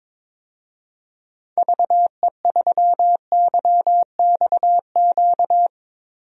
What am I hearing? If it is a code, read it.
Morse code, heard as a keyed tone, about 22 wpm, 700 Hz. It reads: VE3YXQ
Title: DXCC-100 Call Signs - 6 Characters - Repeated 3x - 22wpm